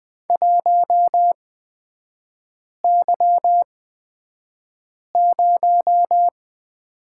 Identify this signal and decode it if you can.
Morse code, heard as a keyed tone, about 20 wpm, spaced as for 5 wpm, 700 Hz. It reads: 1Y0
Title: 255 Lesson 41e - Period - Three Characters 20wpm